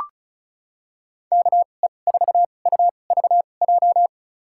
Morse code, heard as a keyed tone, about 34 wpm, spaced as for 25 wpm, 700 Hz. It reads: KE4UVJ
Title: USA Call Signs - 6 Characters - 2x Character Spacing - 35wpm